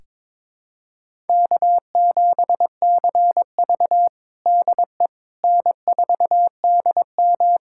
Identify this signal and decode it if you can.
Morse code, heard as a keyed tone, about 22 wpm, 700 Hz. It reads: K7CV DE N4DM
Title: USA Callsign DE Callsign - 4 Characters - 22wpm